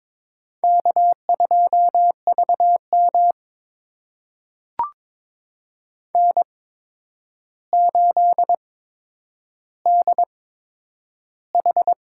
Morse code, heard as a keyed tone, about 22 wpm, spaced as for 9 wpm, 700 Hz. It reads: K2VM N8DH
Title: USA Call Signs - 4 Characters - 8x Character Spacing - 22wpm